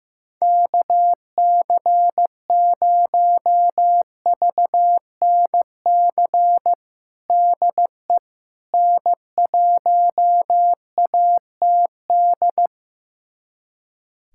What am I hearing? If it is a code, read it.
Morse code, heard as a keyed tone, about 15 wpm, 700 Hz. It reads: KC0VNC DE N1ATD
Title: USA Callsign DE Callsign 15wpm